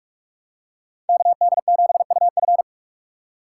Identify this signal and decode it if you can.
Morse code, heard as a keyed tone, about 43 wpm, 700 Hz. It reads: KD7UF
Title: USA Call Signs - 5 Characters - 4x Character Spacing - 45wpm